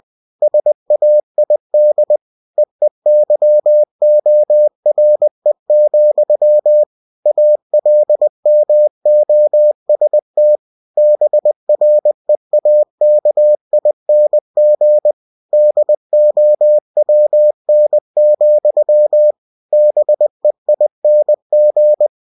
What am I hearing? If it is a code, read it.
Morse code, heard as a keyed tone, about 20 wpm, 600 Hz. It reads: SAID EEYORE, ALMOST BREAKING DOWN, BEING
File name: cw-books / winnie-the-pooh / winnie-20wpm-page-0048